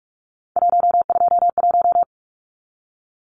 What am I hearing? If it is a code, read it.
Morse code, heard as a keyed tone, about 44 wpm, 700 Hz. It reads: 121
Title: Numbers - 3-Digit 45wpm